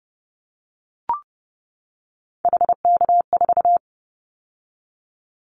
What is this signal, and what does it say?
Morse code, heard as a keyed tone, about 30 wpm, 700 Hz. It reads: HK4
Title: Callsign Prefixes for Top 100 DXCC - 30wpm